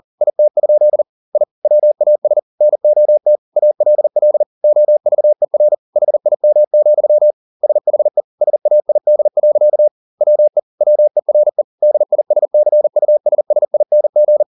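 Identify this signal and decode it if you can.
Morse code, heard as a keyed tone, about 40 wpm, 600 Hz. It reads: IT? I WAS NOT ALL OVER HIM, SHE SAID. WE WERE DISCUSSING
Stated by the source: Morse code at 40 wpm, practice text, CW med Gnister.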